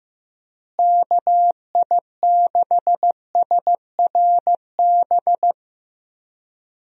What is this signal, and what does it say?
Morse code, heard as a keyed tone, about 16 wpm, 700 Hz. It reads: KI6SRB
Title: USA Call Signs - 6 Characters - Repeated 3x - 15wpm